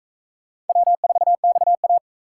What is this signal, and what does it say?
Morse code, heard as a keyed tone, about 38 wpm, 700 Hz. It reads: W4XA